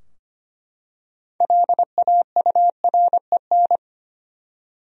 Morse code, heard as a keyed tone, about 25 wpm, 700 Hz. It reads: LAUREN
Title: USA Common Names - 25wpm